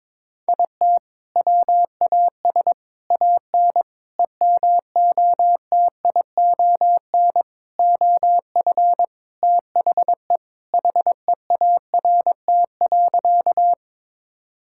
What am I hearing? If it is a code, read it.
Morse code, heard as a keyed tone, about 21 wpm, 700 Hz. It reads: IT WAS AN EMOTION OF THE HEART.
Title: Sentences from Top 1600 Words 22wpm